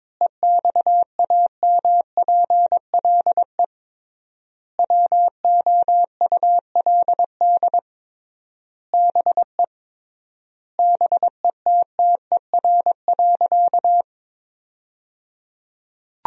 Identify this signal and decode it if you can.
Morse code, heard as a keyed tone, about 21 wpm, 700 Hz. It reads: EXAMPLE WOULD BE BETTER.
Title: 3x Word-Spacing - Sentences from Top 400 Words - 22wpm